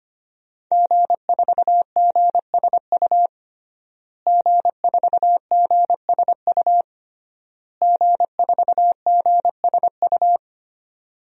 Morse code, heard as a keyed tone, about 25 wpm, 700 Hz. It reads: G4GSU G4GSU G4GSU
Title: DXCC-100 Call Signs - 5 Characters - Repeated 3x - 25wpm